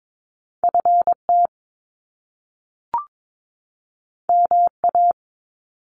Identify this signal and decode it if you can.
Morse code, heard as a keyed tone, about 22 wpm, 700 Hz. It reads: FT MA
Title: Top Two Letter Combinations in Words 22wpm